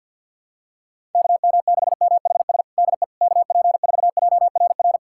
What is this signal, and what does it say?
Morse code, heard as a keyed tone, about 51 wpm, 700 Hz. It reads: KM6MHS DE KP4JRR
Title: USA Callsign DE Callsign - 6 Characters - 50wpm